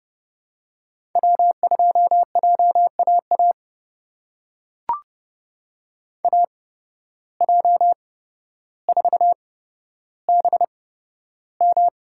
Morse code, heard as a keyed tone, about 30 wpm, spaced as for 13 wpm, 700 Hz. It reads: W2JAA AJ4BM
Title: USA Call Signs - 5 Characters - 8x Character Spacing - 30wpm